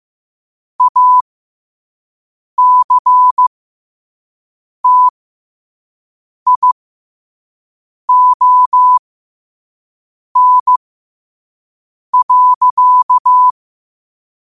Morse code, heard as a keyed tone, about 15 wpm, spaced as for 5 wpm, 1000 Hz. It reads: ACTION.
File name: qotdfiles / 5wpm